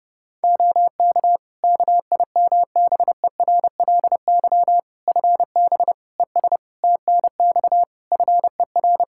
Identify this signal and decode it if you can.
Morse code, heard as a keyed tone, about 30 wpm, 700 Hz. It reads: OK KIMBERLY FB ES TNX FER
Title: Practice QSO - Exchange 2 - Info - 30wpm